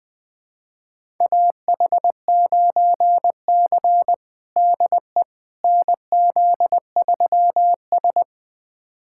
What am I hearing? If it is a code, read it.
Morse code, heard as a keyed tone, about 20 wpm, 700 Hz. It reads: AH9C DE NZ3S
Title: USA Callsign DE Callsign - 4 Characters - 20wpm